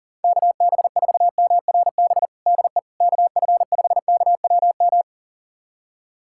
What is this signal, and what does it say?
Morse code, heard as a keyed tone, about 40 wpm, 700 Hz. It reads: KB4MRB DE KF5KWM
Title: USA Callsign DE Callsign 40wpm